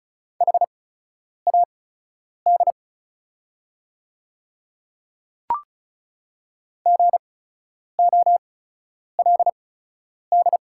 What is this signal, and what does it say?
Morse code, heard as a keyed tone, about 36 wpm, spaced as for 10 wpm, 690 Hz. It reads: HAD GOLD
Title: 8x Character Spacing - Top 500 Words - 35wpm